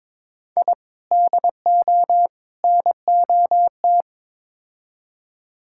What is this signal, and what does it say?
Morse code, heard as a keyed tone, about 22 wpm, 700 Hz. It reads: I DO NOT
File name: practice / 22wpm / Build A Phrase - 5-Word Phrases - 22wpm